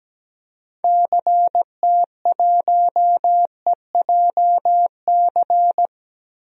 Morse code, heard as a keyed tone, about 17 wpm, 700 Hz. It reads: CT1EJC
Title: DXCC-100 Call Signs - 6 Characters - Repeated 2x - 17wpm